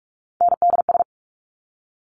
Morse code, heard as a keyed tone, about 46 wpm, 700 Hz. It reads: NDS
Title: Top Three Letter Combinations in Words 45wpm